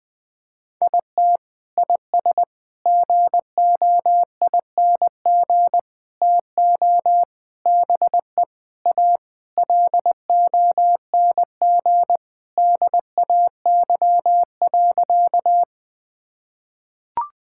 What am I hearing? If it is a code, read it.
Morse code, heard as a keyed tone, about 20 wpm, 700 Hz. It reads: IT IS GOING TO BE A LONG DAY.
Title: Sentences with 5-Word Phrases - Spoken Beginning - 20wpm